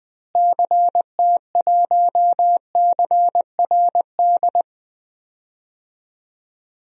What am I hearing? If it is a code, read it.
Morse code, heard as a keyed tone, about 21 wpm, 700 Hz. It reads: CT1CRD
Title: DXCC-100 Call Signs - 6 Characters - Repeated 2x - 20wpm